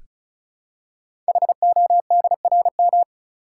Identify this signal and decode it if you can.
Morse code, heard as a keyed tone, about 36 wpm, 700 Hz. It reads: HODRM
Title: Code Groups - 5 Letters - 35wpm